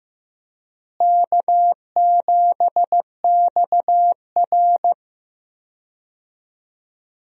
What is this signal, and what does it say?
Morse code, heard as a keyed tone, about 15 wpm, 700 Hz. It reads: K7XR
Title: USA Call Signs - 4 Characters - Repeated 2x - 15wpm